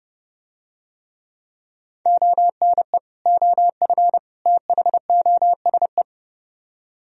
Morse code, heard as a keyed tone, about 30 wpm, 700 Hz. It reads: ONE OF THOSE